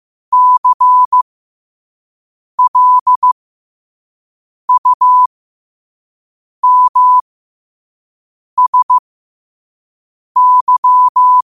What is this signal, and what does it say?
Morse code, heard as a keyed tone, about 15 wpm, spaced as for 5 wpm, 1000 Hz. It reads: CLUMSY
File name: qotdfiles / 5wpm